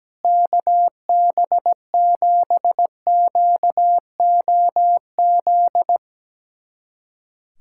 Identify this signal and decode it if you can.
Morse code, heard as a keyed tone, about 17 wpm, 700 Hz. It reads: KB7QOZ
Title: USA Callsign DE Callsign 17wpm